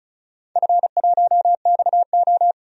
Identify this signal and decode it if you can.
Morse code, heard as a keyed tone, about 36 wpm, 700 Hz. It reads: F1XO
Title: DXCC-100 Call Signs - 4 Characters - 35wpm